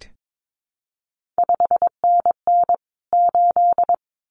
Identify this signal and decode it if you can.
Morse code, heard as a keyed tone, about 22 wpm, 700 Hz. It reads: 5NN 8
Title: CQ WW DX Contest Exchange - 22wpm